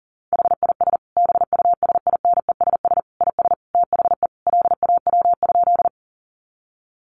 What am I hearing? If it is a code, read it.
Morse code, heard as a keyed tone, about 40 wpm, 700 Hz. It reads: HIS BUSINESS IS THE LAW?
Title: Sentences from Top 600 Words 40wpm